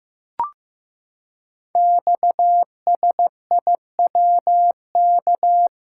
Morse code, heard as a keyed tone, about 15 wpm, 700 Hz. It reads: XSIWK
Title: Code Groups - 5 Letters - 15wpm